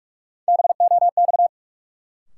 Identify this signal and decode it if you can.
Morse code, heard as a keyed tone, about 46 wpm, 700 Hz. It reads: BOX
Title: Sentences from Top 500 Words - New Word Drill - 45wpm